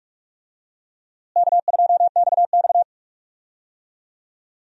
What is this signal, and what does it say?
Morse code, heard as a keyed tone, about 45 wpm, 700 Hz. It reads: K2XX
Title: USA Call Signs - 4 Characters - Repeated 3x - 45wpm